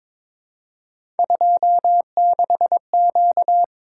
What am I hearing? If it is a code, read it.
Morse code, heard as a keyed tone, about 22 wpm, 700 Hz. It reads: 26Q
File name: practice / 22wpm / Three Letter-Number Combinations - 22wpm